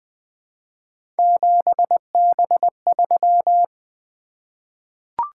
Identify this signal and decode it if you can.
Morse code, heard as a keyed tone, about 20 wpm, 700 Hz. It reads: 7B3
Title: Three Letter-Number Combinations - 20wpm